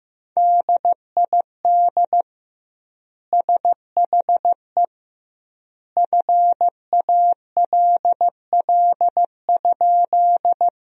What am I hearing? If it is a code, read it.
Morse code, heard as a keyed tone, about 15 wpm, 700 Hz. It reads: DID SHE FALL?
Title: 2x Word-Spacing - Sentences from Top 500 Words - 15wpm